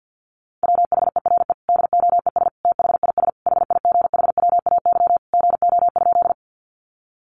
Morse code, heard as a keyed tone, about 50 wpm, 700 Hz. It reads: WHERE DOES THIS HIGHWAY GO?
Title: Sentences from Top 1400 Words 50wpm